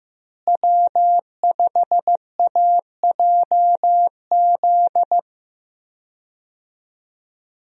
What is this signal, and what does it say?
Morse code, heard as a keyed tone, about 15 wpm, 700 Hz. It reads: W5AJZ
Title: USA Call Signs - 5 Characters - Repeated 3x - 15wpm